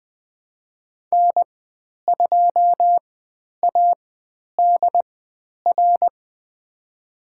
Morse code, heard as a keyed tone, about 20 wpm, spaced as for 10 wpm, 700 Hz. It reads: N2ADR
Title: 095 Lesson 16g - Letter c - Call Signs 20wpm